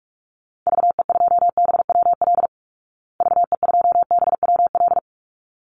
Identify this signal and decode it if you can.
Morse code, heard as a keyed tone, about 44 wpm, 700 Hz. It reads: VE2BWL VE2BWL
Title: Call Signs Repeated 3x - 45wpm